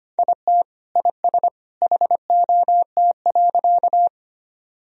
Morse code, heard as a keyed tone, about 25 wpm, 700 Hz. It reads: IT IS HOT.
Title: Sentences from Top 1000 Words - Review Part 3 - 25wpm